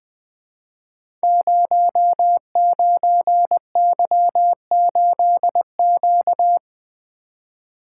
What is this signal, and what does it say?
Morse code, heard as a keyed tone, about 20 wpm, 700 Hz. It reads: 09Y8Q
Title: Code Groups - 5 Letter:Numbers - 20wpm